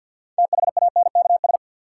Morse code, heard as a keyed tone, about 52 wpm, 700 Hz. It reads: THANKS
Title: Sentences from Top 600 Words - New Word Drill - 50wpm